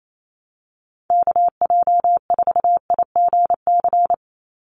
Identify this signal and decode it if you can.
Morse code, heard as a keyed tone, about 28 wpm, 700 Hz. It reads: KJ4IGC
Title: USA Call Signs 28wpm